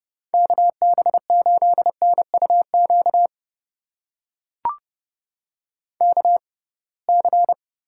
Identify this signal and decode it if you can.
Morse code, heard as a keyed tone, about 30 wpm, spaced as for 23 wpm, 700 Hz. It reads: KB8NUQ KC
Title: USA Call Signs - 6 Characters - 6x Character Spacing - 30wpm